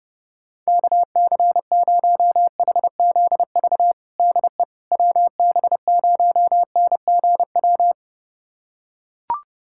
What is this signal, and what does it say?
Morse code, heard as a keyed tone, about 30 wpm, 700 Hz. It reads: KC0HZV DE WB0NGW
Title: USA Callsign DE Callsign - 6 Characters - 30wpm